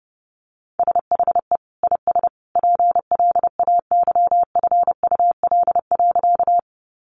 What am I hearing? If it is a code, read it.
Morse code, heard as a keyed tone, about 30 wpm, 700 Hz. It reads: SHE IS PLAYFUL.
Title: Easy Phrases 30wpm